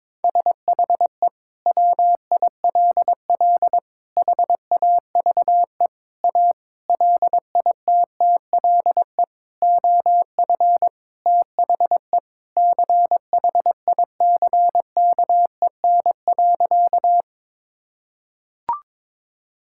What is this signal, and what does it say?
Morse code, heard as a keyed tone, about 22 wpm, 700 Hz. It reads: SHE WILL HAVE A LITTLE OF THE CHICKEN.
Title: Sentences from Top 1000 Words - Review Part 5 - 22wpm